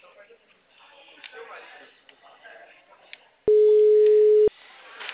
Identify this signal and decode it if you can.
Ringback tone (Europe, 425 Hz), with speech.
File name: out- -2003- - - .159